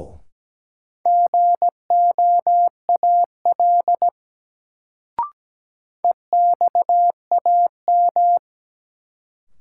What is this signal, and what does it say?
Morse code, heard as a keyed tone, about 17 wpm, 700 Hz. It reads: GOAL EXAM
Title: Sentences from Top 1000 Words - Review Part 5 - New Word Drill - 17wpm